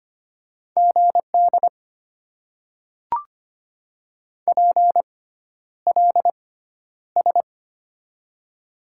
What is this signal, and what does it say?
Morse code, heard as a keyed tone, about 25 wpm, spaced as for 11 wpm, 690 Hz. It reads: GD PLS
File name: Top 90 QSO Elements - 6x Character Spacing - 25wpm